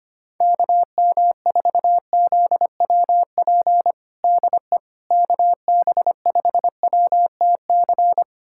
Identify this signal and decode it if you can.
Morse code, heard as a keyed tone, about 25 wpm, 700 Hz. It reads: KM4ZWP DE KB5WTC
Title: USA Callsign DE Callsign 25wpm